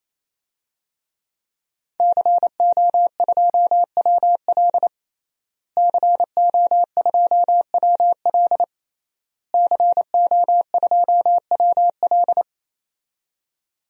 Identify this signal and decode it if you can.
Morse code, heard as a keyed tone, about 27 wpm, 700 Hz. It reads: CO2WL CO2WL CO2WL
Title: DXCC-100 Call Signs - 5 Characters - Repeated 3x - 28wpm